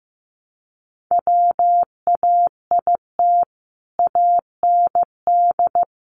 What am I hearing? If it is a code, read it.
Morse code, heard as a keyed tone, about 15 wpm, 700 Hz. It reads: WAIT AND
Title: Sentences from Top 500 Words 15wpm